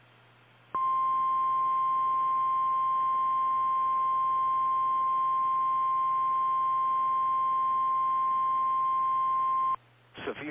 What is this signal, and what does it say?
Emergency Alert System (SAME) tones (the 1050 Hz attention tone), with speech. It is heard from 0.7 to 9.8 s.
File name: eas / KKPRKXPN / KKPRKXPN